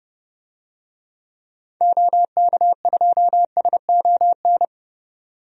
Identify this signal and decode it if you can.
Morse code, heard as a keyed tone, about 32 wpm, 700 Hz. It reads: OK2SON
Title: DXCC-100 Call Signs - 6 Characters - 30wpm